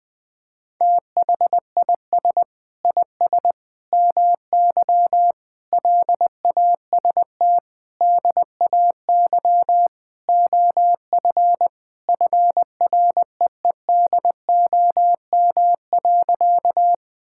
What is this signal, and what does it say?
Morse code, heard as a keyed tone, about 20 wpm, 700 Hz. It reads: THIS IS MY LAST DAY OF FREEDOM.
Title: Sentences from Top 1000 Words - Review Part 5 - 20wpm